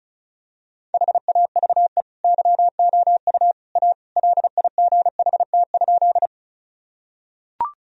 Morse code, heard as a keyed tone, about 34 wpm, 700 Hz. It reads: HAVE YOU A LIGHT?
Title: Sentences from Top 1000 Words - Review Part 1 - 35wpm